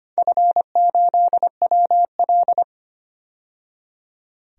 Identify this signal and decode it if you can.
Morse code, heard as a keyed tone, about 25 wpm, 700 Hz. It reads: F8WL
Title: DXCC-100 Call Signs - 4 Characters - 25wpm